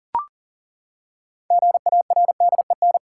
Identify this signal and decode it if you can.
Morse code, heard as a keyed tone, about 40 wpm, 700 Hz. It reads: GARDEN
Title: Six Letter Words - 40wpm